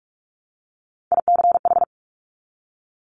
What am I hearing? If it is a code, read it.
Morse code, heard as a keyed tone, about 46 wpm, 700 Hz. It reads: ICH
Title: Top Three Letter Combinations in Words 45wpm